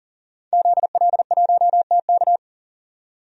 Morse code, heard as a keyed tone, about 40 wpm, 700 Hz. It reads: ZL1TK
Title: DXCC-100 Call Signs - 5 Characters - 40wpm